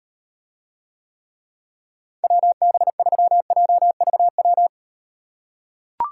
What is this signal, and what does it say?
Morse code, heard as a keyed tone, about 38 wpm, 700 Hz. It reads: WB3JVW